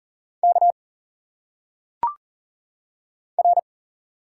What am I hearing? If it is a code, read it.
Morse code, heard as a keyed tone, about 40 wpm, 700 Hz. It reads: K R
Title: Single Letters - Phonetic Pronunciation 40wpm